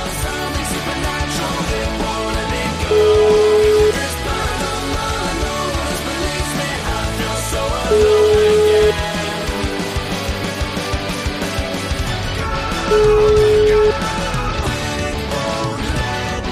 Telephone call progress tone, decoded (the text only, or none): ringback tone